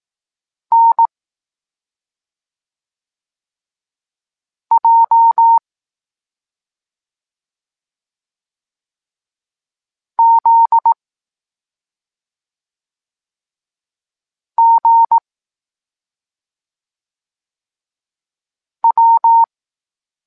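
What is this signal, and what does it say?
Morse code, heard as a keyed tone, about 18 wpm, 920 Hz. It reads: N J Z G W
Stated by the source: Evenly spaced Morse lesson text 21 for learning letters G and Z, from SP9MOA Niepołomicki Klub Krótkofalowców.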